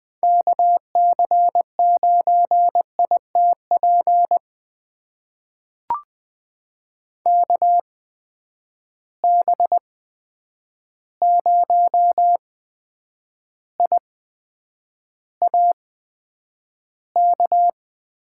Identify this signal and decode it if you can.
Morse code, heard as a keyed tone, about 20 wpm, spaced as for 9 wpm, 700 Hz. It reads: KC9ITP KB0IAK
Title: USA Call Signs - 6 Characters - 8x Character Spacing - 20wpm